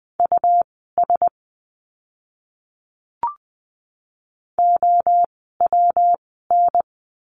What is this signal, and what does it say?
Morse code, heard as a keyed tone, about 20 wpm, spaced as for 14 wpm, 700 Hz. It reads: US OWN
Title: 192 Lesson 31f - Letter v - Words 20wpm